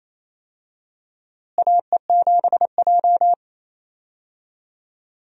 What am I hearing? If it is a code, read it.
Morse code, heard as a keyed tone, about 28 wpm, 700 Hz. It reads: AE7J
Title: USA Call Signs - 4 Characters - Repeated 2x - 28wpm